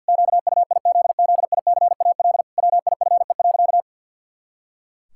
Morse code, heard as a keyed tone, about 49 wpm, 700 Hz. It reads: QUIZZICAL WIFE.
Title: Pangram Challenge v2 - 50wpm